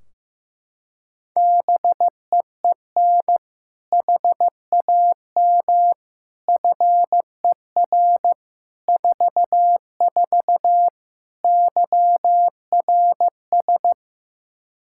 Morse code, heard as a keyed tone, about 15 wpm, 700 Hz. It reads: BEEN HAM FER 44 YRS
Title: Practice QSO - Exchange 3 - More Info - 15wpm